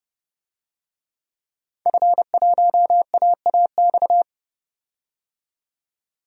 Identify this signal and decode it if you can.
Morse code, heard as a keyed tone, about 29 wpm, 700 Hz. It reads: F1AAX
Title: DXCC-100 Call Signs - 5 Characters - 30wpm